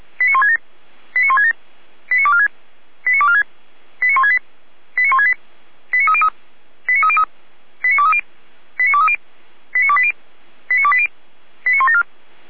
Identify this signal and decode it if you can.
Five-tone selective call (ZVEI), heard as quick runs of five tones, tones of about 70 ms. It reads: 89167 89167 89246 89246 89168 89168 89303 89303 89230 89230 89280 89280 89174